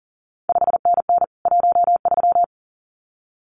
Morse code, heard as a keyed tone, about 39 wpm, 700 Hz. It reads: 5NN 13
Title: CQ WW DX Contest Exchange Encore - 40wpm